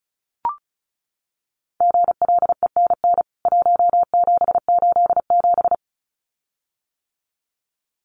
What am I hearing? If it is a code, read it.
Morse code, heard as a keyed tone, about 34 wpm, 700 Hz. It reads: GLENN 1787
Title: CWT Exchange 35wpm